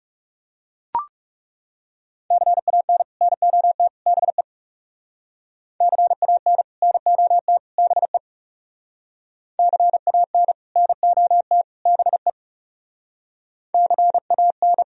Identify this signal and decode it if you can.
Morse code, heard as a keyed tone, about 39 wpm, 700 Hz. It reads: CAN NOT BE CAN NOT BE CAN NOT BE CAN